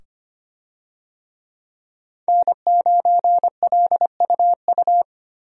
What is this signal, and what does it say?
Morse code, heard as a keyed tone, about 25 wpm, 700 Hz. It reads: N9LUU
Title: USA Call Signs - 5 Characters - 25wpm